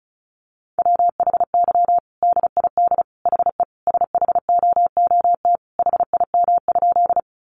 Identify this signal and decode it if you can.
Morse code, heard as a keyed tone, about 34 wpm, 700 Hz. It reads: WHY DID HE SHOOT HIM?
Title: Sentences from Top 1200 Words 35wpm